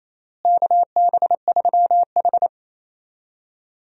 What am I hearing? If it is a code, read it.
Morse code, heard as a keyed tone, about 27 wpm, 700 Hz. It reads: KB3H
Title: USA Call Signs - 4 Characters - Repeated 3x - 28wpm